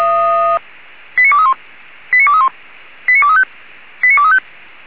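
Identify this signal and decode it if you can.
Five-tone selective call (ZVEI), heard as quick runs of five tones, tones of about 70 ms. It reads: 89231 89231 89235 89235
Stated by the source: ZVEI five-tone selective call received by SDR.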